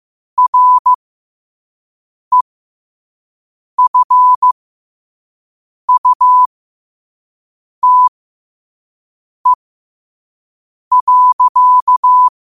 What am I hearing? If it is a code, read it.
Morse code, heard as a keyed tone, about 15 wpm, spaced as for 5 wpm, 1000 Hz. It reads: REFUTE.
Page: qotd-at-5-wpm